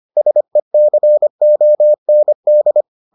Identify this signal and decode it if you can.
Morse code, heard as a keyed tone, about 25 wpm, 600 Hz. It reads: SECOND